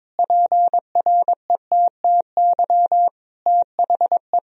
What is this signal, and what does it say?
Morse code, heard as a keyed tone, about 22 wpm, 700 Hz. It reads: PRETTY THE
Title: Sentences from Top 1000 Words - Review Part 4 - 22wpm